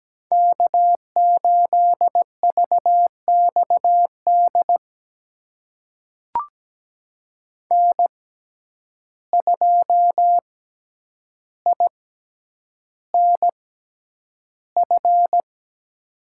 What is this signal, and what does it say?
Morse code, heard as a keyed tone, about 17 wpm, spaced as for 9 wpm, 700 Hz. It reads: K8VXD N2INF
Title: USA Call Signs - 5 Characters - 6x Character Spacing - 17wpm